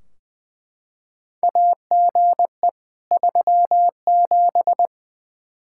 Morse code, heard as a keyed tone, about 20 wpm, 700 Hz. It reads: AGE 37
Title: Practice QSO - Exchange 3 - More Info - 20wpm